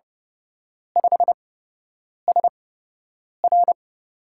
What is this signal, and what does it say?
Morse code, heard as a keyed tone, about 29 wpm, spaced as for 8 wpm, 700 Hz. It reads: 5SR
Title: USA Call Signs - 5 Characters - 8x Character Spacing - 30wpm